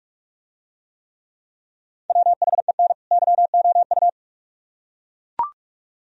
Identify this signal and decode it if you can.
Morse code, heard as a keyed tone, about 40 wpm, 700 Hz. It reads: WHEN YOU